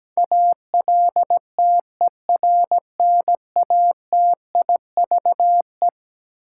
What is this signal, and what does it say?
Morse code, heard as a keyed tone, about 17 wpm, 700 Hz. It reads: ALTERNATIVE
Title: Sentences from Top 1000 Words - Review Part 4 - New Word Drill - 17wpm